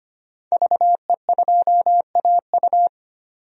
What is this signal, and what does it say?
Morse code, heard as a keyed tone, about 25 wpm, 700 Hz. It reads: VE2AU
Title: DXCC-100 Call Signs - 5 Characters - 25wpm